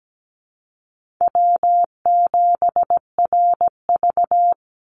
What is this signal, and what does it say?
Morse code, heard as a keyed tone, about 17 wpm, 700 Hz. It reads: W7RV
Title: NA SOTA Chaser Callsigns 17wpm